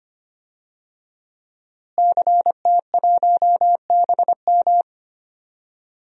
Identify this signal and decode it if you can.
Morse code, heard as a keyed tone, about 25 wpm, 700 Hz. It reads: CT1BM
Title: DXCC-100 Call Signs - 5 Characters - Repeated 2x - 25wpm